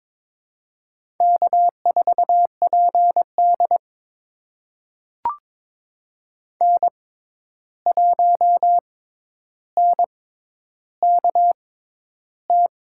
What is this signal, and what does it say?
Morse code, heard as a keyed tone, about 22 wpm, spaced as for 10 wpm, 700 Hz. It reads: K4PD N1NKT
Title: USA Call Signs - 6x Character Spacing - 22wpm